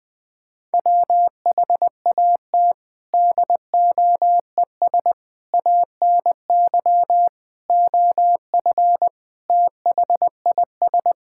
Morse code, heard as a keyed tone, about 20 wpm, 700 Hz. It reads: WHAT DOES ANY OF THIS
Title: Sentences with 9-Word Phrases - 20wpm